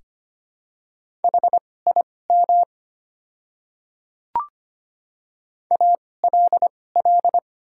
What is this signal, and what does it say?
Morse code, heard as a keyed tone, about 25 wpm, spaced as for 18 wpm, 690 Hz. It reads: HIM ALL
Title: 2x Character Spacing - Top 200 Words - 25wpm